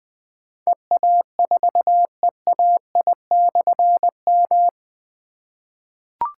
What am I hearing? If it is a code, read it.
Morse code, heard as a keyed tone, about 20 wpm, 700 Hz. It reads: EA4EAI/M
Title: Difficult Call Signs 20wpm